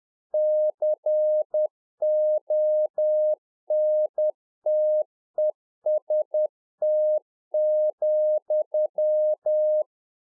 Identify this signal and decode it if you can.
Morse code, heard as a keyed tone, about 10 wpm, 610 Hz. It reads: CONTEST,
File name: 2011-04-mp3 / RICK JJ0MPI 10WPM